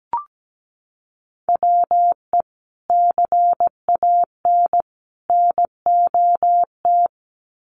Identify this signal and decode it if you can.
Morse code, heard as a keyed tone, about 17 wpm, 700 Hz. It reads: WE CAN NOT
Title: Sets of 3 Words 17wpm